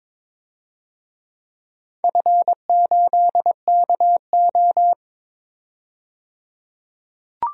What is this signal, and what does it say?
Morse code, heard as a keyed tone, about 22 wpm, 700 Hz. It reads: F8KO
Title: DXCC-100 Call Signs - 4 Characters - 22wpm